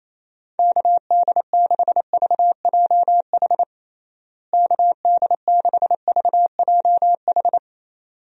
Morse code, heard as a keyed tone, about 28 wpm, 700 Hz. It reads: KD6VJH KD6VJH
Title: USA Call Signs - 6 Characters - Repeated 2x - 28wpm